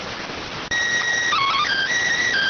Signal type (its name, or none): SSTV (slow-scan television)